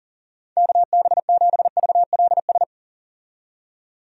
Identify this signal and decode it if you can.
Morse code, heard as a keyed tone, about 40 wpm, 700 Hz. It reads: KB7VLS
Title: USA Call Signs - 6 Characters - 40wpm